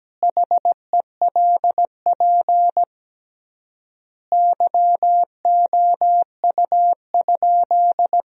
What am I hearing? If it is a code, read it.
Morse code, heard as a keyed tone, about 17 wpm, 700 Hz. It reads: HELP YOU?
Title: 3x Word-Spacing - Sentences from Top 200 Words - 17wpm